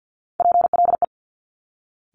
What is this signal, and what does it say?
Morse code, heard as a keyed tone, about 50 wpm, 700 Hz. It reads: PLE
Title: Top Three Letter Combinations in Words 50wpm